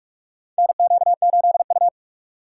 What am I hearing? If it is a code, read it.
Morse code, heard as a keyed tone, about 44 wpm, 700 Hz. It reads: NQ8U
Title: USA Call Signs - 4 Characters - 4x Character Spacing - 45wpm